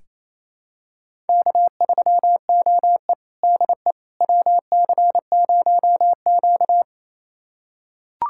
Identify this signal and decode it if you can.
Morse code, heard as a keyed tone, about 28 wpm, 700 Hz. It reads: K3OE DE WC0Q
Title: USA Callsign DE Callsign - 4 Characters - 28wpm